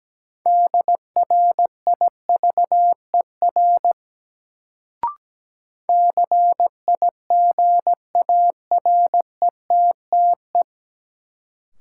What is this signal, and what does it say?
Morse code, heard as a keyed tone, about 17 wpm, 700 Hz. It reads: DRIVER CIGARETTE
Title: Sentences from Top 1000 Words - New Word Drill - 17wpm